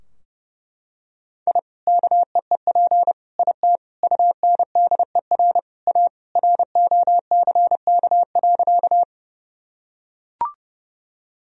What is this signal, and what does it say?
Morse code, heard as a keyed tone, about 30 wpm, 700 Hz. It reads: I KEEP IT UNDER A ROCK.
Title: Sentences from Top 1000 Words - Review Part 2 - 30wpm